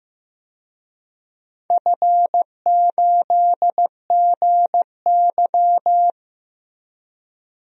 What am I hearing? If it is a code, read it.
Morse code, heard as a keyed tone, about 15 wpm, 700 Hz. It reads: F8GY
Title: DXCC-100 Call Signs - 4 Characters - Repeated 2x - 15wpm